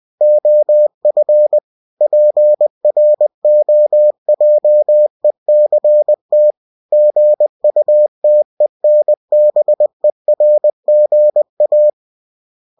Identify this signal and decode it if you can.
Morse code, heard as a keyed tone, about 20 wpm, 600 Hz. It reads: OF PROJECT GUTENBERGA
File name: cw-books / winnie-the-pooh / winnie-20wpm-page-0120